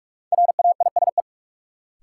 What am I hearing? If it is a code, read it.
Morse code, heard as a keyed tone, about 46 wpm, 700 Hz. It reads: RAISE
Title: Sentences from Top 1400 Words - New Word Drill - 45wpm